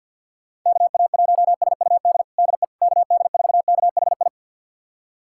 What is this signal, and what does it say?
Morse code, heard as a keyed tone, about 52 wpm, 700 Hz. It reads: KA1SUD DE KD4KHI
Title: USA Callsign DE Callsign - 6 Characters - 50wpm